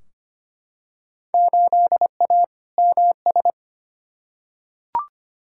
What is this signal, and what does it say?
Morse code, heard as a keyed tone, about 25 wpm, 700 Hz. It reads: 8A MS
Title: Field Day Exchanges - 25wpm